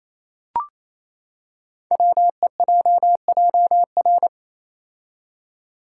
Morse code, heard as a keyed tone, about 27 wpm, 700 Hz. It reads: WEJJR
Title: Code Groups - 5 Letters - 28wpm